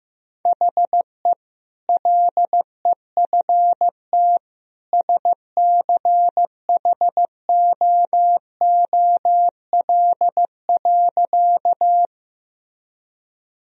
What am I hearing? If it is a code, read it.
Morse code, heard as a keyed tone, about 15 wpm, 700 Hz. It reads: HE LEFT SCHOOL.
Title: Sentences from Top 1000 Words - Review Part 1 - 15wpm